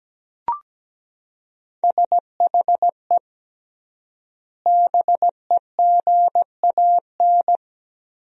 Morse code, heard as a keyed tone, about 17 wpm, 700 Hz. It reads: SHE BEGAN
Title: 3x Word-Spacing - Sentences from Top 500 Words - 17wpm